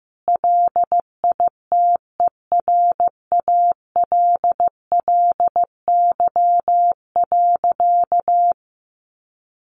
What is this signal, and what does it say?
Morse code, heard as a keyed tone, about 15 wpm, 700 Hz. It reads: LITERALLY.
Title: Sentences from Top 1400 Words 15wpm